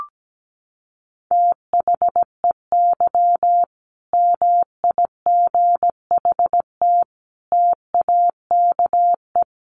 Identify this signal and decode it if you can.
Morse code, heard as a keyed tone, about 17 wpm, 700 Hz. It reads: THEY MIGHT TAKE
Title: Sentences from Top 1000 Words 17wpm